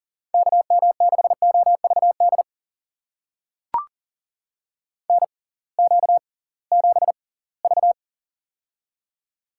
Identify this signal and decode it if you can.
Morse code, heard as a keyed tone, about 38 wpm, spaced as for 23 wpm, 700 Hz. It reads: KM6OVD NQ7V
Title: USA Call Signs - 6x Character Spacing - 40wpm